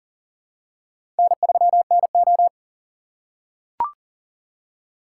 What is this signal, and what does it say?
Morse code, heard as a keyed tone, about 38 wpm, 700 Hz. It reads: N3NO
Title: USA Call Signs - 4 Characters - 4x Character Spacing - 40wpm